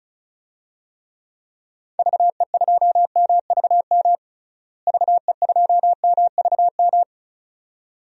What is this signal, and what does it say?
Morse code, heard as a keyed tone, about 36 wpm, 700 Hz. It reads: VE2MVM VE2MVM
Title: DXCC-100 Call Signs - 6 Characters - Repeated 2x - 35wpm